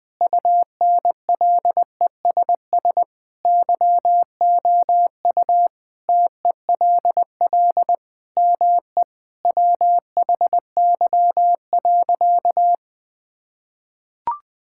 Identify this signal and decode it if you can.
Morse code, heard as a keyed tone, about 20 wpm, 700 Hz. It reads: UNLESS YOU TELL ME WHY.
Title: Sentences from Top 1000 Words - Review Part 6 - 20wpm